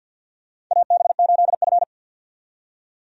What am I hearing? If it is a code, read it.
Morse code, heard as a keyed tone, about 51 wpm, 700 Hz. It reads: AB8F
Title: USA Call Signs - 4 Characters - 50wpm